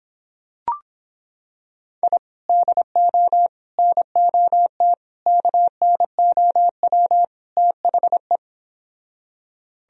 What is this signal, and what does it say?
Morse code, heard as a keyed tone, about 26 wpm, 700 Hz. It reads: I DO NOT KNOW THE